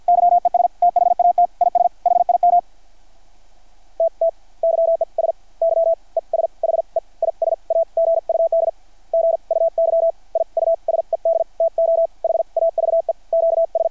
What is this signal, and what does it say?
Morse code, heard as a keyed tone, about 50 wpm, spaced as for 43 wpm, 660 Hz. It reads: YES THAT IS HIM T T YES YES HE IS A GUD GUY I USED TO HAVE QS
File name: archived / CRSnet